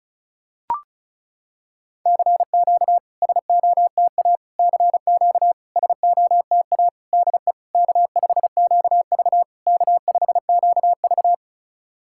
Morse code, heard as a keyed tone, about 34 wpm, 690 Hz. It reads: CQ SOTA CQ SOTA DE K5QV K5QV
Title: SOTA QSO Practice 35wpm